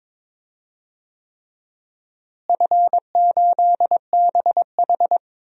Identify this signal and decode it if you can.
Morse code, heard as a keyed tone, about 22 wpm, 700 Hz. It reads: F8BH
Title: DXCC-100 Call Signs - 4 Characters - Repeated 3x - 22wpm